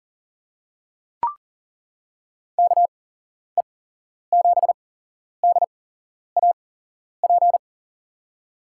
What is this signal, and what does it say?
Morse code, heard as a keyed tone, about 40 wpm, spaced as for 11 wpm, 700 Hz. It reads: KE7DAP
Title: USA Call Signs - 6 Characters - 8x Character Spacing - 40wpm